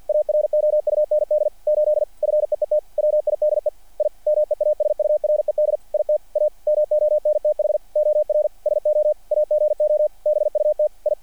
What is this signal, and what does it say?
Morse code, heard as a keyed tone, about 48 wpm, 600 Hz. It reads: AROUND 8 FEET WIDE I MEASURED IT A MONTH OR SO AGO BUT I
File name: archived / CRSnet